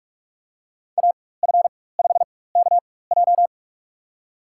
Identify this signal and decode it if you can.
Morse code, heard as a keyed tone, about 46 wpm, spaced as for 21 wpm, 700 Hz. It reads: AF5KJ
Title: USA Call Signs - 5 Characters - 4x Character Spacing - 45wpm